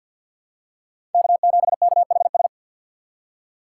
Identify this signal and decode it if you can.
Morse code, heard as a keyed tone, about 47 wpm, 700 Hz. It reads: K7KHS
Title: USA Call Signs - 5 Characters - 6x Character Spacing - 50wpm